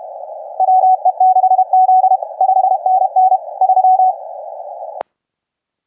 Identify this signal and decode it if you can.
Morse code, heard as a keyed tone, about 30 wpm, 750 Hz, with radio noise. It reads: WE6Z 5NN 3
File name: cqww cw 10 minutes